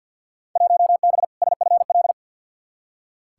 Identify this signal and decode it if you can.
Morse code, heard as a keyed tone, about 47 wpm, 700 Hz. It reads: 1B SFL
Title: Field Day Exchanges Encore - 50wpm